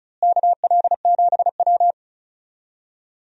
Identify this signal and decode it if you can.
Morse code, heard as a keyed tone, about 36 wpm, 700 Hz. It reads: KL7W
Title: DXCC-100 Call Signs - 4 Characters - Repeated 3x - 35wpm